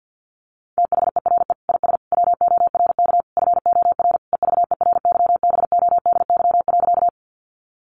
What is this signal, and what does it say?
Morse code, heard as a keyed tone, about 50 wpm, 700 Hz. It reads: THERE IS WORK FOR EVERYBODY.
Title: Sentences from Top 1400 Words 50wpm